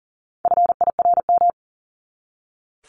Morse code, heard as a keyed tone, about 40 wpm, 700 Hz. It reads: FIRM
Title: Top 1251-1500 Words 40wpm